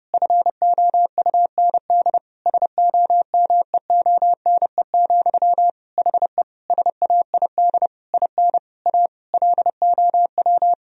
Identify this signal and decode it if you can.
Morse code, heard as a keyed tone, about 32 wpm, 700 Hz. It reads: FOUND SOMEONE, HE SAID IN A LOW
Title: Sentences from Top 1000 Words - Review Part 1 - 30wpm